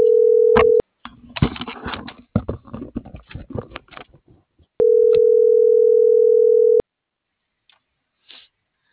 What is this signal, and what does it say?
Ringback tone (North America), 440 + 480 Hz.